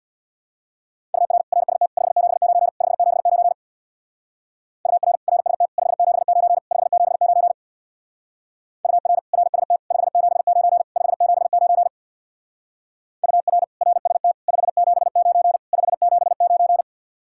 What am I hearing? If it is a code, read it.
Morse code, heard as a keyed tone, about 62 wpm, 700 Hz. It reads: UR RST 579 579 UR RST 579 579 UR RST 579 579 UR RST 579 579
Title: Speed-Racing Common QSO Phrases Encore - 50wpm